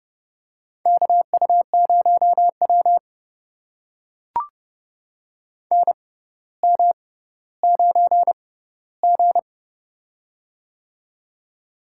Morse code, heard as a keyed tone, about 30 wpm, spaced as for 15 wpm, 700 Hz. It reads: KU0W NM9G
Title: USA Call Signs - 4 Characters - 6x Character Spacing - 30wpm